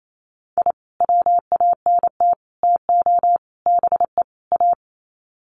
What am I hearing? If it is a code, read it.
Morse code, heard as a keyed tone, about 28 wpm, 700 Hz. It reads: I WANT TO BE A
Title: Sets of 5 Words Encore 28wpm